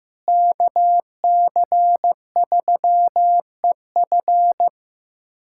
Morse code, heard as a keyed tone, about 15 wpm, 700 Hz. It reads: KC3EF
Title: K1USN SST QSO Practice - 15wpm